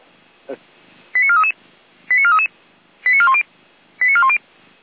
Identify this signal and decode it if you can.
Five-tone selective call (ZVEI), heard as quick runs of five tones, tones of about 70 ms, with speech. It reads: 89433 89433 89410 89410